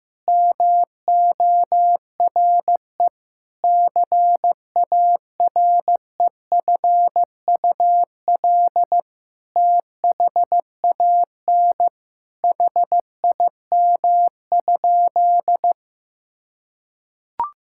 Morse code, heard as a keyed tone, about 16 wpm, 700 Hz. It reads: MORE CAREFUL THAN HIM?
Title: Sentences from Top 1600 Words 15wpm